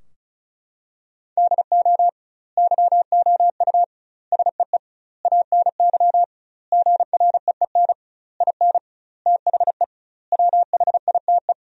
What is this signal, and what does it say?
Morse code, heard as a keyed tone, about 34 wpm, 700 Hz. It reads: DO YOU SEE ANY GREEN IN THE WHITE
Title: 2x Word-Spacing - Sentences from Top 500 Words - 35wpm